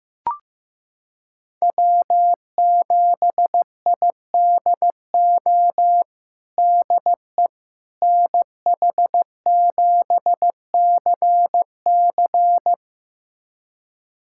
Morse code, heard as a keyed tone, about 15 wpm, 700 Hz. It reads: W7IDO DE NH7CC
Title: USA Callsign DE Callsign - 5 Characters - 15wpm